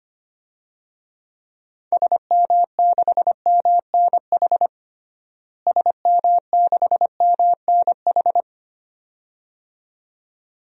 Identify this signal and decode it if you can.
Morse code, heard as a keyed tone, about 25 wpm, 700 Hz. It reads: SM6MNH SM6MNH
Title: DXCC-100 Call Signs - 6 Characters - Repeated 2x - 25wpm